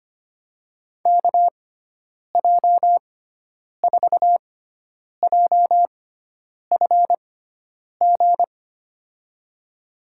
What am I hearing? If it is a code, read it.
Morse code, heard as a keyed tone, about 25 wpm, spaced as for 9 wpm, 700 Hz. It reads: KJ4JFG
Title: USA Call Signs - 6x Character Spacing - 25wpm